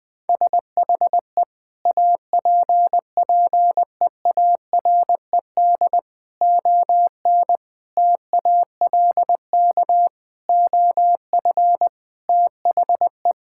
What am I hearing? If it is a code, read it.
Morse code, heard as a keyed tone, about 21 wpm, 700 Hz. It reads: SHE APPEARED ON TALK OF THE
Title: Sentences with 5-Word Phrases - 20wpm